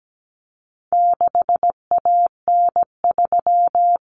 Morse code, heard as a keyed tone, about 17 wpm, 700 Hz. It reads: 6AN3
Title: Common Cut Numbers - 4-Digit 17wpm